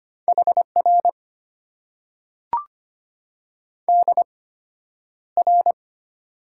Morse code, heard as a keyed tone, about 25 wpm, spaced as for 11 wpm, 700 Hz. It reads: HR DR
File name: Top 90 QSO Elements - 8x Character Spacing - 25wpm